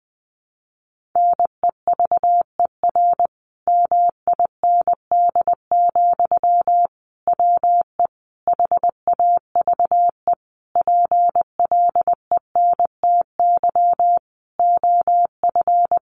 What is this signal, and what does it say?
Morse code, heard as a keyed tone, about 19 wpm, 700 Hz. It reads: NEVER MIND, WE HAVE PLENTY OF
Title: Sentences from Top 1300 Words 20wpm